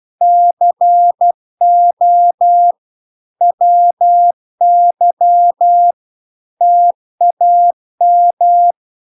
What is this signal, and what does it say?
Morse code, heard as a keyed tone, about 12 wpm, 700 Hz. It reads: CO WY TAM